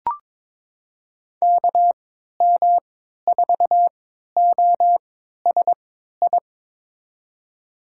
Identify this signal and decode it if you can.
Morse code, heard as a keyed tone, about 22 wpm, spaced as for 13 wpm, 700 Hz. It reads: KM4OSI
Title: USA Call Signs - 6 Characters - 3x Character Spacing - 22wpm